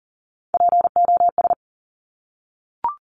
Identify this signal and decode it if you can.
Morse code, heard as a keyed tone, about 42 wpm, 700 Hz. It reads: POS
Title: Top Three Letter Combinations in Words 40wpm